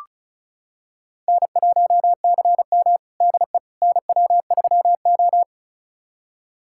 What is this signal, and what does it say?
Morse code, heard as a keyed tone, about 36 wpm, 700 Hz. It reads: N1CM DE NW3O
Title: USA Callsign DE Callsign - 4 Characters - 35wpm